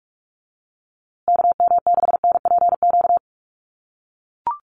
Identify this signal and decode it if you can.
Morse code, heard as a keyed tone, about 44 wpm, 700 Hz. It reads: KM6NPQ
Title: USA Call Signs Repeated 3x 45wpm